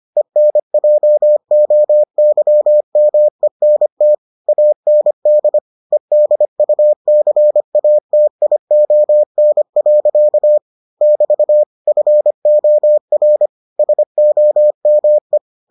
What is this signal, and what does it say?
Morse code, heard as a keyed tone, about 25 wpm, 600 Hz. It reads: ENJOYMENT AND EDUCATION. <BT> FOR SOME